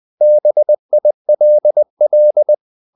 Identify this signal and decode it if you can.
Morse code, heard as a keyed tone, about 20 wpm, 600 Hz. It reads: BILL
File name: the old man and the sea / the old man and the sea part 1 20wpm